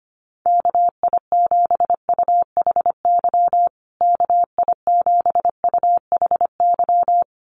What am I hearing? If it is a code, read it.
Morse code, heard as a keyed tone, about 25 wpm, 700 Hz. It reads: KI7UHY KI7UHY
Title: USA Call Signs Repeated 2x 25wpm